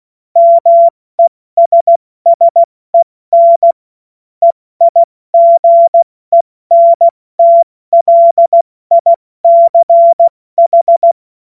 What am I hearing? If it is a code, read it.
Morse code, heard as a keyed tone, about 16 wpm, 680 Hz. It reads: MESSEN EIGENTLICH
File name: Morsekurs / Witze / Witz 09. Dezember 2025 70 F80